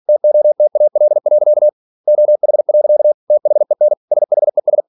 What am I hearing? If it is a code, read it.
Morse code, heard as a keyed tone, about 46 wpm, 600 Hz. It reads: TOTAL. OH. THEN SHES